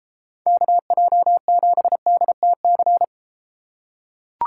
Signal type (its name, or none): Morse code